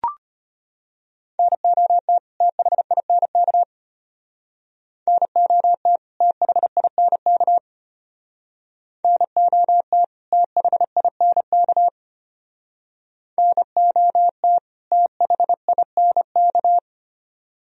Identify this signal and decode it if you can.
Morse code, heard as a keyed tone, about 31 wpm, 700 Hz. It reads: NOT THINK NOT THINK NOT THINK NOT THINK
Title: Speed-Racing - Sets of 2 Words 25wpm